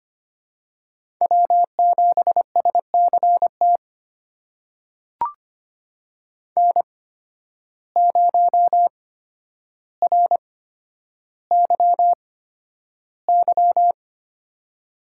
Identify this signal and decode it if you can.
Morse code, heard as a keyed tone, about 25 wpm, spaced as for 11 wpm, 700 Hz. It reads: W7SCT N0RYY
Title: USA Call Signs - 5 Characters - 8x Character Spacing - 25wpm